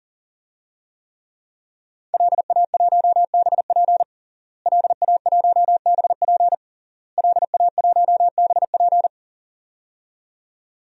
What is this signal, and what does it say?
Morse code, heard as a keyed tone, about 40 wpm, 700 Hz. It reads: LA1BP LA1BP LA1BP
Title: DXCC-100 Call Signs - 5 Characters - Repeated 3x - 40wpm